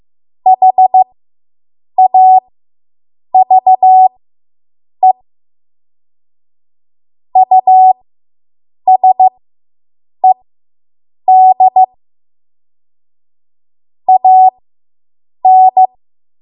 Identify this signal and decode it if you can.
Morse code, heard as a keyed tone, about 15 wpm, spaced as for 7 wpm, 750 Hz. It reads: HAVE USED AN